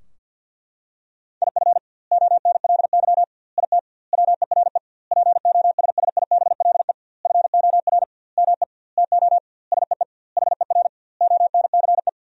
Morse code, heard as a keyed tone, about 47 wpm, 700 Hz. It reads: IF ONLY IT WERE POSSIBLE FOR ME TO SEE HER ONCE
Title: 2x Word-Spacing - Sentences from Top 500 Words - 50wpm